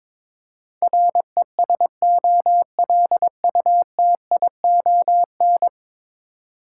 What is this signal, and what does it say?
Morse code, heard as a keyed tone, about 22 wpm, 700 Hz. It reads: RESOLUTION
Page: morse-code-ninja-practice-22wpm